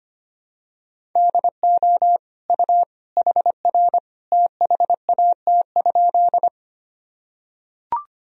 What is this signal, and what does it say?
Morse code, heard as a keyed tone, about 25 wpm, 700 Hz. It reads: DO U HR THAT?
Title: Sentences from Top 300 Words and Abbreviations 25wpm